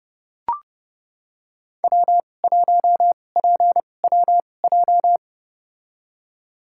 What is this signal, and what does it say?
Morse code, heard as a keyed tone, about 30 wpm, spaced as for 22 wpm, 700 Hz. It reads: W1PWJ
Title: USA Call Signs - 5 Characters - 2x Character Spacing - 30wpm